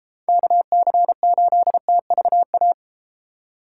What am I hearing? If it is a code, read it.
Morse code, heard as a keyed tone, about 30 wpm, 700 Hz. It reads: KC8TVA